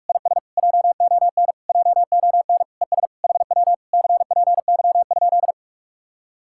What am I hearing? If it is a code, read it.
Morse code, heard as a keyed tone, about 58 wpm, 700 Hz. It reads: IS JON JON ES HW CPY?